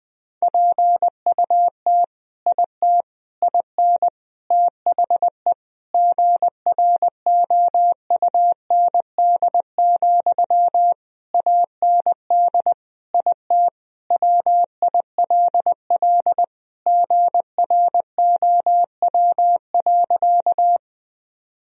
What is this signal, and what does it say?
Morse code, heard as a keyed tone, about 20 wpm, 700 Hz. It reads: PUT IT IN THE GROUND, AND IT WILL GROW.
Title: Sentences from Top 1000 Words - Review Part 2 - 20wpm